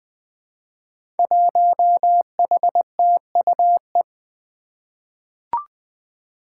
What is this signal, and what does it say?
Morse code, heard as a keyed tone, about 20 wpm, 700 Hz. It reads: 1HTUE
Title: Five Challenging Letter-Number Combinations 20wpm